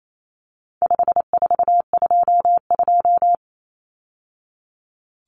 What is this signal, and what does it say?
Morse code, heard as a keyed tone, about 28 wpm, 700 Hz. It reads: 5422
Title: Numbers - 4-Digit 28wpm